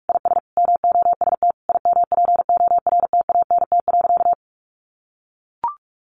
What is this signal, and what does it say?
Morse code, heard as a keyed tone, about 44 wpm, 700 Hz. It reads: IS MOST IMPORTANT.
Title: Sentences from Top 1300 Words 45wpm